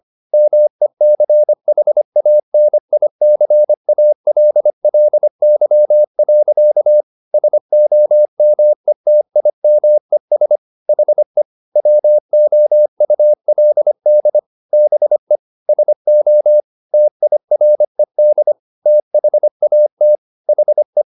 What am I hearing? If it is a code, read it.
Morse code, heard as a keyed tone, about 25 wpm, 600 Hz. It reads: MECHANICALLY. SOMETIMES HE WOULD BE SO TIRED THAT HE
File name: the old man and the sea part 1 25wpm